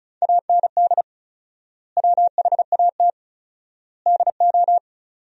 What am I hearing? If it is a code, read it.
Morse code, heard as a keyed tone, about 34 wpm, 700 Hz. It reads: AND WHAT DO
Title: 4x Word-Spacing - Sentences from Top 100 Words - 35wpm